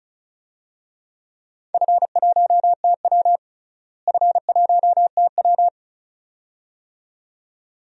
Morse code, heard as a keyed tone, about 36 wpm, 700 Hz. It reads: F1TW F1TW
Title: DXCC-100 Call Signs - 4 Characters - Repeated 2x - 35wpm